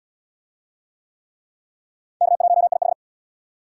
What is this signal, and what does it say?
Morse code, heard as a keyed tone, about 76 wpm, 700 Hz. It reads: K1IH